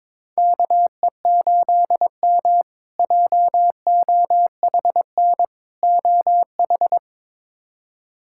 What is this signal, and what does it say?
Morse code, heard as a keyed tone, about 22 wpm, 700 Hz. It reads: KE8M JOHN OH
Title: K1USN SST QSO Practice - 22wpm